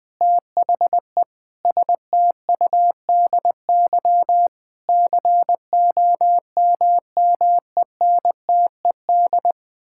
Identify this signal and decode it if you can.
Morse code, heard as a keyed tone, about 20 wpm, 700 Hz. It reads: THE STUDY COMMENTED
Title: Sentences with 7-Word Phrases - 20wpm